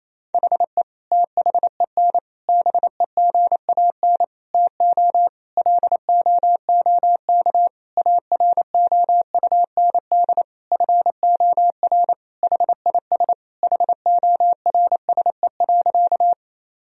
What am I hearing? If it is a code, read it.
Morse code, heard as a keyed tone, about 28 wpm, 700 Hz. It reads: HE THEN BEGAN TO LOOK AROUND FOR HIS HORSE.
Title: Sentences from Top 1000 Words - Review Part 3 - 28wpm